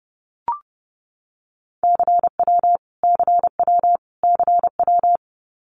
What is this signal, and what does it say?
Morse code, heard as a keyed tone, about 30 wpm, 700 Hz. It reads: CW CW CW
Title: Abbreviations Repeated 3x - 30wpm